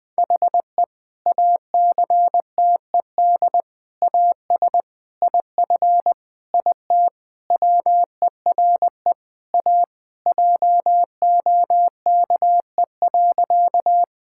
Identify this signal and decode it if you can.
Morse code, heard as a keyed tone, about 20 wpm, 700 Hz. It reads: HE ACTED AS IF IT WERE A JOKE.
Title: Sentences with 5-Word Phrases - 20wpm